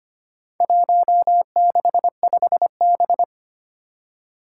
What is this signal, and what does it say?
Morse code, heard as a keyed tone, about 25 wpm, 700 Hz. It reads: 165B
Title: Four Challenging Letter-Number Combinations 25wpm